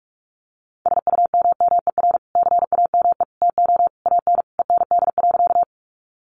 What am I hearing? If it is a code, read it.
Morse code, heard as a keyed tone, about 44 wpm, 700 Hz. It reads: SUMMER CAME TO AN END.
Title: Sentences from Top 1100 Words 45wpm